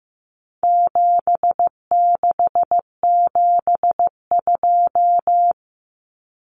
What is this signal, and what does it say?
Morse code, heard as a keyed tone, about 15 wpm, 700 Hz. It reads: 7672
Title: Common Cut Numbers - 4-Digit 15wpm